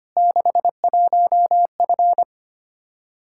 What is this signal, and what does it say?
Morse code, heard as a keyed tone, about 25 wpm, 700 Hz. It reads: 61F
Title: Practice QSO - Exchange 2 - Info - 25wpm